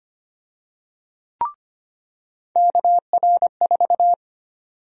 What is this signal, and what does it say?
Morse code, heard as a keyed tone, about 25 wpm, 700 Hz. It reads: KR4
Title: USA Call Sign Prefixes - 25wpm